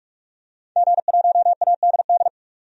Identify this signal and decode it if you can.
Morse code, heard as a keyed tone, about 46 wpm, 700 Hz. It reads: G1ADD
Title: DXCC-100 Call Signs - 5 Characters - 45wpm